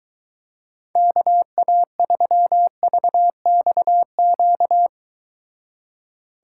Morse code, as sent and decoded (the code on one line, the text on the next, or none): -.- .- ...-- ...- -..- --.-
KA3VXQ